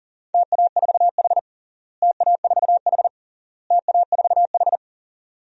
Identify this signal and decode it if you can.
Morse code, heard as a keyed tone, about 40 wpm, 700 Hz. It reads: TA4H TA4H TA4H
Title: DXCC-100 Call Signs - 4 Characters - Repeated 3x - 40wpm